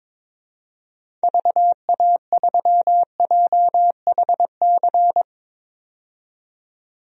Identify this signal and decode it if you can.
Morse code, heard as a keyed tone, about 22 wpm, 700 Hz. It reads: VA3JHC
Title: DXCC-100 Call Signs - 6 Characters - 22wpm